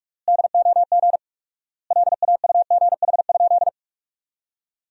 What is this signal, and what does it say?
Morse code, heard as a keyed tone, about 46 wpm, 700 Hz. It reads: DOG LAUGH?
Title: 4x Word-Spacing - Sentences from Top 500 Words - 45wpm